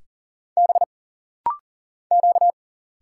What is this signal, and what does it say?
Morse code, heard as a keyed tone, about 40 wpm, 700 Hz. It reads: B Q
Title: Single Letters - Rapid-Fire 40wpm